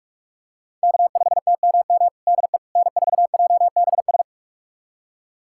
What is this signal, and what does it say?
Morse code, heard as a keyed tone, about 44 wpm, 700 Hz. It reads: K5TMM DE N4JBS
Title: USA Callsign DE Callsign - 5 Characters - 45wpm